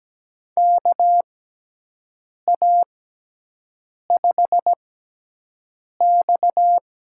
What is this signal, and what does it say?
Morse code, heard as a keyed tone, about 17 wpm, spaced as for 6 wpm, 700 Hz. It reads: KA5X
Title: USA Call Signs - 6x Character Spacing - 17wpm